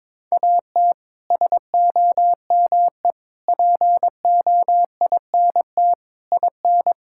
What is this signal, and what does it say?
Morse code, heard as a keyed tone, about 22 wpm, 700 Hz. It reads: AT SOME POINT IN
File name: practice / 22wpm / Sets of 6 Words - 22wpm